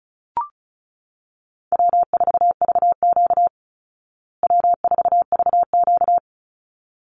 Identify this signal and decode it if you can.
Morse code, heard as a keyed tone, about 34 wpm, 700 Hz. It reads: W4VQ W4VQ
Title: Easy Call Signs Repeated 2x - 35wpm